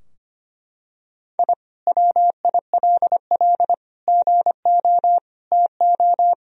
Morse code, heard as a keyed tone, about 25 wpm, 700 Hz. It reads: I WILL GO TO
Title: Sentences from Top 1000 Words - Review Part 4 - 25wpm